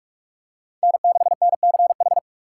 Morse code, heard as a keyed tone, about 45 wpm, 700 Hz. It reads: N6NCH
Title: USA Call Signs - 5 Characters - 4x Character Spacing - 45wpm